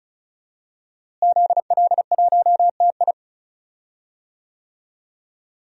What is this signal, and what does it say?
Morse code, heard as a keyed tone, about 36 wpm, 700 Hz. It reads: ZL1TI
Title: DXCC-100 Call Signs - 5 Characters - Repeated 2x - 35wpm